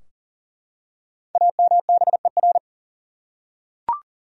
Morse code, heard as a keyed tone, about 40 wpm, 700 Hz. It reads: AMBER
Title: USA Common Names - Max 5 Letters - 40wpm